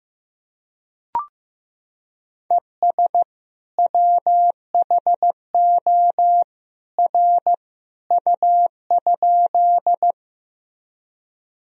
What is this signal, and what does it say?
Morse code, heard as a keyed tone, about 15 wpm, 700 Hz. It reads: ES WHO R U?
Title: Sentences from Top 100 Words and Abbreviations 15wpm